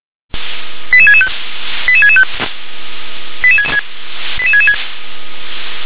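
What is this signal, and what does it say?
Five-tone selective call (ZVEI), heard as quick runs of five tones, tones of about 70 ms. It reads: 88665 88665 88666
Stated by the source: ZVEI five-tone selective call, address 88084, received on an SDR.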